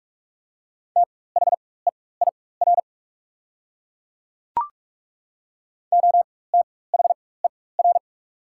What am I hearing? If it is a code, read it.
Morse code, heard as a keyed tone, about 43 wpm, spaced as for 21 wpm, 700 Hz. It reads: THEIR OTHER
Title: 4x Character Spacing - Top 500 Words - 45wpm